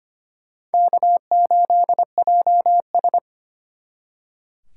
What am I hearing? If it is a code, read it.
Morse code, heard as a keyed tone, about 32 wpm, 700 Hz. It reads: K8JS